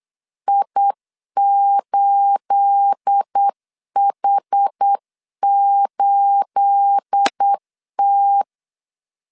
Morse code, heard as a keyed tone, about 8 wpm, 800 Hz. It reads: I8H8T
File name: LEZIONI / 9 28 8wpm 3512